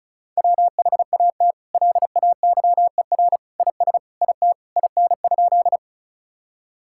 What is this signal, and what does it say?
Morse code, heard as a keyed tone, about 36 wpm, 700 Hz. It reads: WHAT LAYER IS IT IN?
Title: Sentences from Top 1600 Words 35wpm